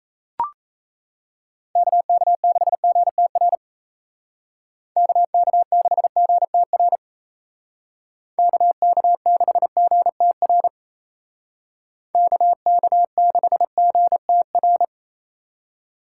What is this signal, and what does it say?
Morse code, heard as a keyed tone, about 35 wpm, 700 Hz. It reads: KK6GTR KK6GTR KK6GTR KK6GTR
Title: Speed Racing - USA Call Signs 28wpm